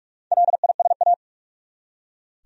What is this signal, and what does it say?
Morse code, heard as a keyed tone, about 46 wpm, 700 Hz. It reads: LISA
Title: USA Common Names - Max 4 Letters - 45wpm